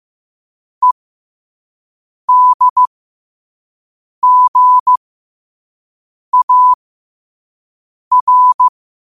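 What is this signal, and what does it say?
Morse code, heard as a keyed tone, about 15 wpm, spaced as for 5 wpm, 1000 Hz. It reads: EDGAR
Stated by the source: Morse practice at 5 wpm.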